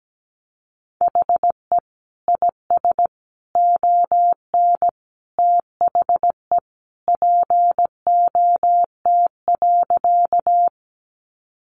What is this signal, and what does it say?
Morse code, heard as a keyed tone, about 17 wpm, 700 Hz. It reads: HE IS ON THE POT.
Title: Sentences from Top 1000 Words 17wpm